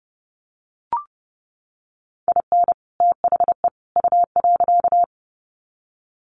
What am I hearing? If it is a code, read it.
Morse code, heard as a keyed tone, about 30 wpm, 700 Hz. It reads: IN THE U.
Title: Sets of 3 Words 30wpm